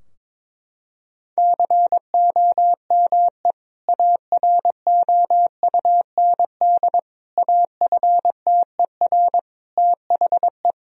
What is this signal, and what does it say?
Morse code, heard as a keyed tone, about 22 wpm, 700 Hz. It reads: COME AROUND AFTER THE
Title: Sentences from Top 1000 Words - Review Part 3 - 22wpm